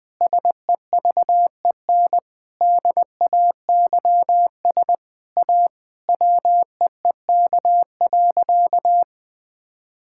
Morse code, heard as a keyed tone, about 20 wpm, 700 Hz. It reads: SEVEN DAYS A WEEK.
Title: Sentences with 7-Word Phrases - Spoken Beginning - 20wpm